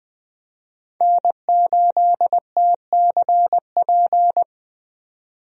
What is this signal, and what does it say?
Morse code, heard as a keyed tone, about 20 wpm, 700 Hz. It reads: N8TCP
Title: USA Call Signs - 5 Characters - 20wpm